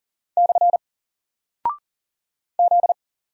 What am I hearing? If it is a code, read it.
Morse code, heard as a keyed tone, about 40 wpm, 700 Hz. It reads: / Z
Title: Single Character (Letter:Number:Punctuation:Prosign) 40wpm